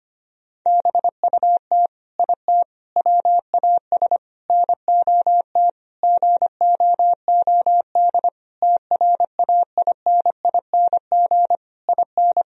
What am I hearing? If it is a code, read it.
Morse code, heard as a keyed tone, about 25 wpm, 700 Hz. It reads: BUT IT WAS NOT GOOD TRAINING IN
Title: Sentences from Top 1000 Words - Review Part 4 - 25wpm